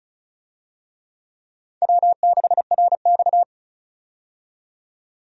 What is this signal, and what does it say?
Morse code, heard as a keyed tone, about 36 wpm, 700 Hz. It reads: W6RX
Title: USA Call Signs - 4 Characters - 35wpm